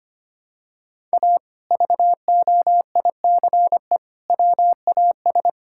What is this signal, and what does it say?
Morse code, heard as a keyed tone, about 25 wpm, 700 Hz. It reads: A VOICE WAS
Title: Sentences from Top 1000 Words - Review Part 5 - 25wpm